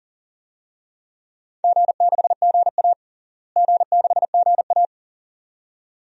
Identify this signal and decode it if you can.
Morse code, heard as a keyed tone, about 37 wpm, 700 Hz. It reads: G6GA G6GA
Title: DXCC-100 Call Signs - 4 Characters - Repeated 2x - 40wpm